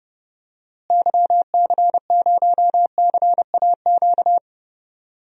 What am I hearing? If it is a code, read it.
Morse code, heard as a keyed tone, about 30 wpm, 700 Hz. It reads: YC0CAQ
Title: DXCC-100 Call Signs - 6 Characters - Repeated 2x - 30wpm